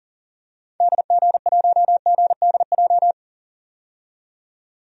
Morse code, heard as a keyed tone, about 40 wpm, 700 Hz. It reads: DG1GDJ
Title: DXCC-100 Call Signs - 6 Characters - 40wpm